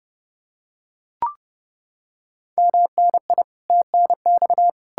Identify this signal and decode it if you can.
Morse code, heard as a keyed tone, about 30 wpm, 700 Hz. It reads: MNI TNX